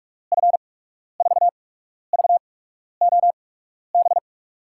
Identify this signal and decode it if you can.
Morse code, heard as a keyed tone, about 44 wpm, spaced as for 12 wpm, 700 Hz. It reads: F4VOB
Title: USA Call Signs - 6 Characters - 8x Character Spacing - 45wpm